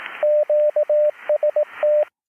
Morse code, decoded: QST